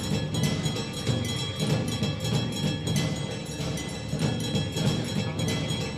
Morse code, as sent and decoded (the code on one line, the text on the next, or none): none